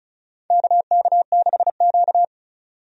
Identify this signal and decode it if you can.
Morse code, heard as a keyed tone, about 34 wpm, 700 Hz. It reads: KK6Q
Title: USA Call Signs - 4 Characters - 35wpm